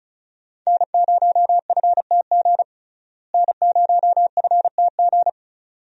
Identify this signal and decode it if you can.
Morse code, heard as a keyed tone, about 36 wpm, 700 Hz. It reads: N0FTG N0FTG
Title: USA Call Signs - 5 Characters - Repeated 2x - 35wpm